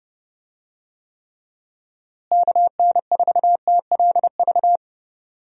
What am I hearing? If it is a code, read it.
Morse code, heard as a keyed tone, about 32 wpm, 700 Hz. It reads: KN4TLV
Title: USA Call Signs - 6 Characters - Repeated 2x - 30wpm